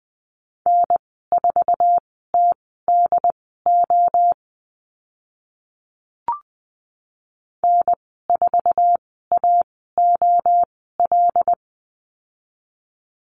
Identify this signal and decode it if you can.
Morse code, heard as a keyed tone, about 20 wpm, spaced as for 14 wpm, 700 Hz. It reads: N4TDO N4AOL
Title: 082 Lesson 14g - Number 2 - Call Signs 20wpm